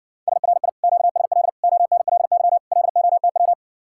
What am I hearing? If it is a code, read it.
Morse code, heard as a keyed tone, about 67 wpm, 700 Hz. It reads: SRI QSL ONLY LOTW